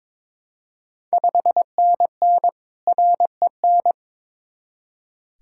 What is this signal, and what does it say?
Morse code, heard as a keyed tone, about 21 wpm, 700 Hz. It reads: 5NN REN
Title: Ontario QSO Party Exchange 22wpm